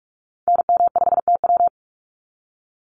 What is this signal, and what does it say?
Morse code, heard as a keyed tone, about 44 wpm, 700 Hz. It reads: NM5TW
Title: NA SOTA Activator Callsigns 45wpm